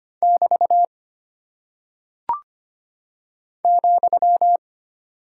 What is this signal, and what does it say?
Morse code, heard as a keyed tone, about 25 wpm, 700 Hz. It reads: <BT> ,
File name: practice / 25wpm / Single Punctuation-Prosign - 25wpm